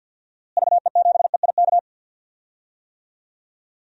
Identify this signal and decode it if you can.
Morse code, heard as a keyed tone, about 46 wpm, 700 Hz. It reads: VE7EIK
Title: DXCC-100 Call Signs - 6 Characters - Repeated 3x - 50wpm